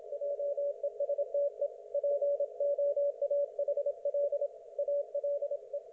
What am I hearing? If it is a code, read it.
Morse code, heard as a keyed tone, about 27 wpm, 560 Hz. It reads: JESTE POAHL ALE
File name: Archiv / CUC179 / ok1fhi cul179